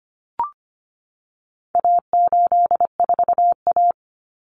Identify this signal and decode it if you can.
Morse code, heard as a keyed tone, about 25 wpm, 700 Hz. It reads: A84A
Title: Common Cut Numbers - 4-Digit 25wpm